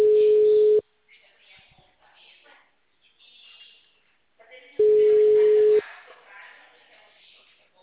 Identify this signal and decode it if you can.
Ringback tone (Europe, 425 Hz), with speech.
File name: out- -2055- - - .9873